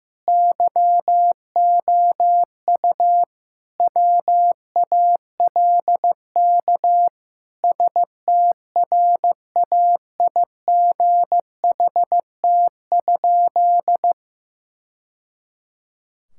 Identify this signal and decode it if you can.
Morse code, heard as a keyed tone, about 16 wpm, 700 Hz. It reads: YOU WALK STRAIGHT?
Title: Sentences from Top 1000 Words - Review Part 6 - 15wpm